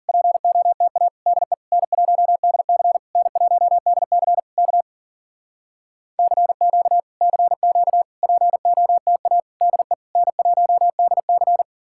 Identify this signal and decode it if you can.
Morse code, heard as a keyed tone, about 52 wpm, 700 Hz. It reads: POTA DE N1DC N1DC K CQ CQ POTA DE N1DC